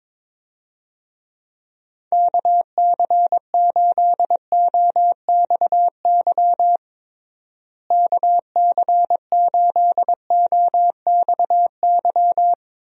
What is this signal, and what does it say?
Morse code, heard as a keyed tone, about 22 wpm, 700 Hz. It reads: KC8OXY KC8OXY
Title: USA Call Signs - 6 Characters - Repeated 3x - 22wpm